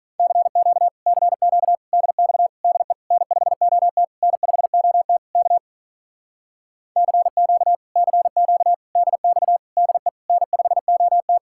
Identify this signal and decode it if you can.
Morse code, heard as a keyed tone, about 46 wpm, 700 Hz. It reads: CQ CQ DX DE N5OT N5OT K CQ CQ DX DE N5OT